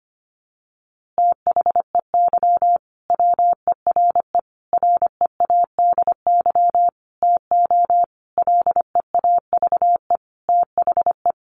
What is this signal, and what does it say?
Morse code, heard as a keyed tone, about 25 wpm, 700 Hz. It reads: THEY WERE READY TO LEAVE THE
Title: A Sound Of Thunder 25wpm